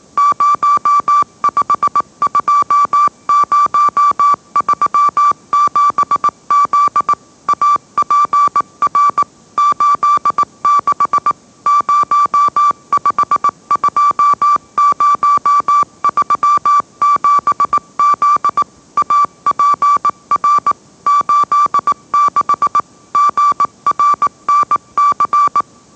Morse code, heard as a keyed tone, about 18 wpm, 1160 Hz. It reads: 052037Z APR 86 052037Z APR 86 GRNC
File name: mrx / mrx081 20wpm